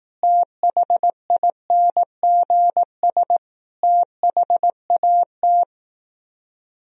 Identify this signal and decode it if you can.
Morse code, heard as a keyed tone, about 18 wpm, 700 Hz. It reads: THINGS THAT
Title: Speed-Racing - Sets of 5 Words 15wpm